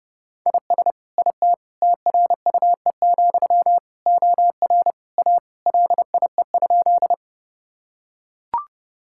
Morse code, heard as a keyed tone, about 32 wpm, 700 Hz. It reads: IS IT TRUE, OR A LIE?
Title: Sentences from Top 1600 Words 30wpm